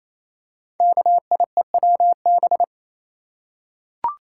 Morse code, heard as a keyed tone, about 29 wpm, 700 Hz. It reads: KIEWB
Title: Code Groups - 5 Letters - 28wpm